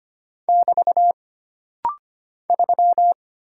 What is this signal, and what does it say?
Morse code, heard as a keyed tone, about 25 wpm, 700 Hz. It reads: <BT> 3
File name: Single Character (Letter:Number:Punctuation:Prosign) - Rapid-Fire - 25wpm